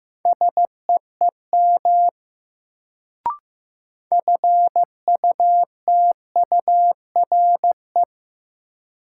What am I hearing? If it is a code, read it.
Morse code, heard as a keyed tone, about 16 wpm, 700 Hz. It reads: SEEM FUTURE
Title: Sets of 3 Words - New Word Drill - 15wpm